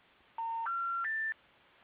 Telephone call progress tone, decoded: special information tone